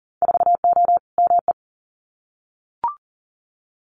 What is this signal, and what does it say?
Morse code, heard as a keyed tone, about 41 wpm, 700 Hz. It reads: 4O ME
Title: Winter Field Day Exchanges Encore 40wpm